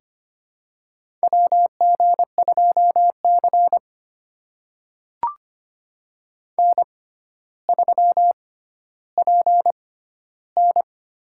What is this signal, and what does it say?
Morse code, heard as a keyed tone, about 25 wpm, spaced as for 13 wpm, 700 Hz. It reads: WG2C N3PN
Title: USA Call Signs - 4 Characters - 6x Character Spacing - 25wpm